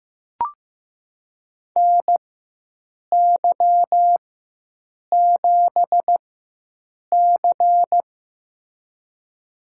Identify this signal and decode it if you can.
Morse code, heard as a keyed tone, about 15 wpm, spaced as for 7 wpm, 700 Hz. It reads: NY7C
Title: USA Call Signs - 4 Characters - 4x Character Spacing - 15wpm